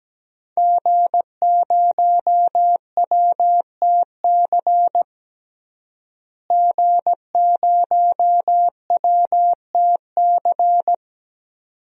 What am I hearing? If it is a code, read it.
Morse code, heard as a keyed tone, about 17 wpm, 700 Hz. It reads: G0WTC G0WTC
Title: DXCC-100 Call Signs - 5 Characters - Repeated 2x - 17wpm